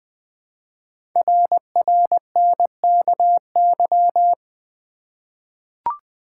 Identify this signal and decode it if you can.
Morse code, heard as a keyed tone, about 20 wpm, 700 Hz. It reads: RRNKY
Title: Code Groups - 5 Letters - 20wpm